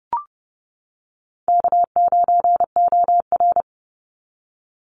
Morse code, heard as a keyed tone, about 30 wpm, 700 Hz. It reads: K9OR
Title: Easy Call Signs - 30wpm